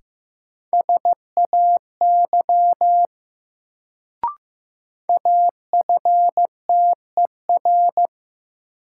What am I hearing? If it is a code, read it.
Morse code, heard as a keyed tone, about 15 wpm, 700 Hz. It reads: SAY AFTER
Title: Sentences from Top 200 Words - New Word Drill - 15wpm